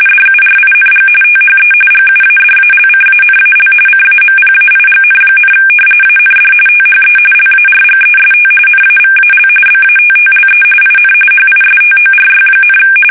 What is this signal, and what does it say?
RTTY, about 45.45 baud. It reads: DURING THE GUADALCANAL CAMPAIGN, AND THE LARGEST NAVAL BATTLE IN HISTORY,
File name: fsk850 / usnavy1